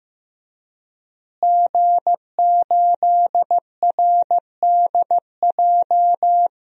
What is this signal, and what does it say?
Morse code, heard as a keyed tone, about 15 wpm, 700 Hz. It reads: G8RDJ
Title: DXCC-100 Call Signs - 5 Characters - Repeated 2x - 15wpm